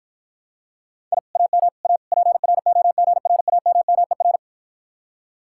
Morse code, heard as a keyed tone, about 52 wpm, 700 Hz. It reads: I AM A PROGRAMMER